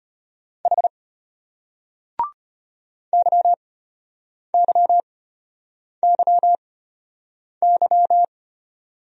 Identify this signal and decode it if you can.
Morse code, heard as a keyed tone, about 34 wpm, spaced as for 8 wpm, 700 Hz. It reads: H YYYY